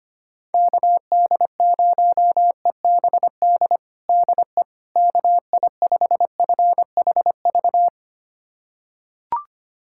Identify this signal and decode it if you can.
Morse code, heard as a keyed tone, about 25 wpm, 700 Hz. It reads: KD0EBD DE KI5FHV
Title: USA Callsign DE Callsign - 6 Characters - 25wpm